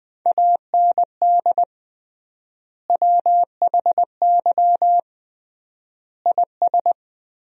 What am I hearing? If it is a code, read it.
Morse code, heard as a keyed tone, about 20 wpm, 700 Hz. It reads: AND WHY IS
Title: 3x Word-Spacing - Sentences from Top 200 Words - 20wpm